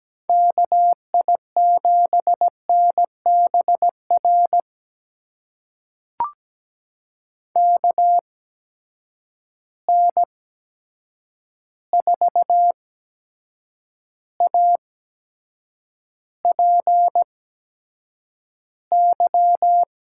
Morse code, heard as a keyed tone, about 17 wpm, spaced as for 7 wpm, 700 Hz. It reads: KI7NBR KN4APY
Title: USA Call Signs - 6 Characters - 8x Character Spacing - 17wpm